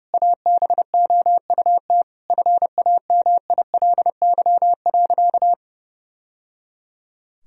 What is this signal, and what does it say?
Morse code, heard as a keyed tone, about 30 wpm, 700 Hz. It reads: ABOUT FAMILY.
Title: Sentences from Top 1000 Words - Review Part 3 - 30wpm